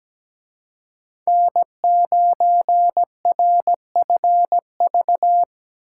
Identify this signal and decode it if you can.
Morse code, heard as a keyed tone, about 20 wpm, 700 Hz. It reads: N9RFV
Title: Speed Racing - USA 1x3 Call Signs 17wpm